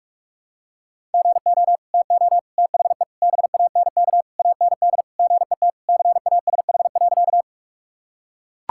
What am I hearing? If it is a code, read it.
Morse code, heard as a keyed tone, about 41 wpm, 700 Hz. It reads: GO TO THE BANK AND GET CASH.
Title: Sentences from Top 1000 Words - Review Part 5 - 45wpm